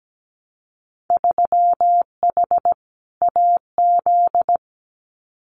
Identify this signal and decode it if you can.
Morse code, heard as a keyed tone, about 17 wpm, 700 Hz. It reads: 3H AZ
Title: Winter Field Day Exchanges Encore 17wpm